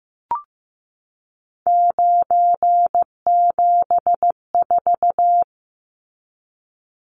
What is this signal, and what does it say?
Morse code, heard as a keyed tone, about 15 wpm, 700 Hz. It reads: 974
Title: Numbers - 3-Digit 15wpm